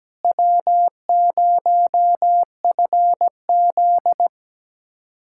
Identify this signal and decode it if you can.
Morse code, heard as a keyed tone, about 17 wpm, 700 Hz. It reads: W0FZ
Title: USA Callsign DE Callsign - 4 Characters - 17wpm